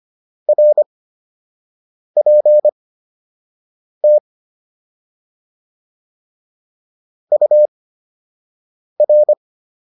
Morse code, heard as a keyed tone, about 25 wpm, spaced as for 6 wpm, 600 Hz. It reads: RPT UR
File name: QSO 203 06